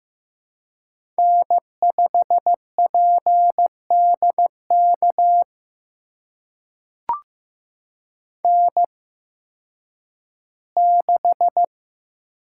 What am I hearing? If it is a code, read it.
Morse code, heard as a keyed tone, about 15 wpm, spaced as for 6 wpm, 700 Hz. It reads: N5PD KN6
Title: USA Call Signs - 8x Character Spacing - 15wpm